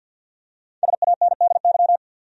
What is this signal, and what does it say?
Morse code, heard as a keyed tone, about 53 wpm, 700 Hz. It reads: SANDY
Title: USA Common Names - Max 5 Letters - 50wpm